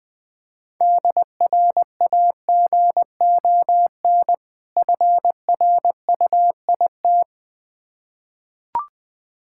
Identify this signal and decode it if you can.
Morse code, heard as a keyed tone, about 20 wpm, 700 Hz. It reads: DRAGON FRUIT
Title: Difficult Words with Hints - 20wpm